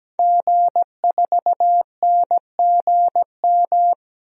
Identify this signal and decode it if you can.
Morse code, heard as a keyed tone, about 17 wpm, 700 Hz. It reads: G4NGM
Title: DXCC-100 Call Signs - 5 Characters - 17wpm